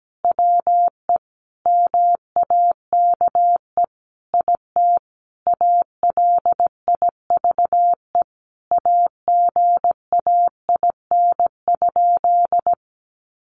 Morse code, heard as a keyed tone, about 17 wpm, 700 Hz. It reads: WE MAKE IT ALIVE AGAIN?
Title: A Sound Of Thunder 17wpm